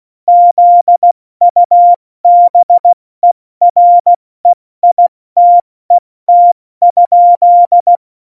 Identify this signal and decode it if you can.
Morse code, heard as a keyed tone, about 16 wpm, 700 Hz. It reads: ZUBEREITET?